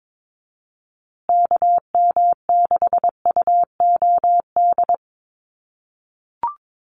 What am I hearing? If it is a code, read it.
Morse code, heard as a keyed tone, about 22 wpm, 700 Hz. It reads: KM6UOD
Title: USA Call Signs Repeated 2x 22wpm